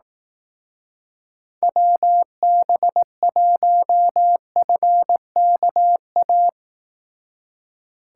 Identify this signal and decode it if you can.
Morse code, heard as a keyed tone, about 18 wpm, 700 Hz. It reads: WB1FKA
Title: Speed Racing - USA Call Signs 15wpm